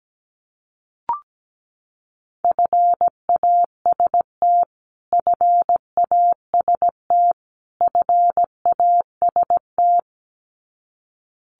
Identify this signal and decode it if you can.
Morse code, heard as a keyed tone, about 17 wpm, 700 Hz. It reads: FAST FAST FAST
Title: Top 500 Words Repeated 17wpm